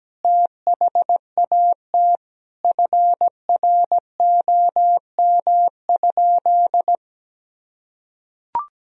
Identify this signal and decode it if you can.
Morse code, heard as a keyed tone, about 17 wpm, 700 Hz. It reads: THAT FROM?
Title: Sentences from Top 1000 Words - Review Part 4 - 17wpm